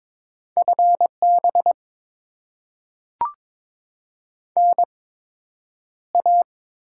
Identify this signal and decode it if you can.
Morse code, heard as a keyed tone, about 22 wpm, 700 Hz. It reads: F B N A
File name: Top 20 QSO Elements - 8x Character Spacing - 22wpm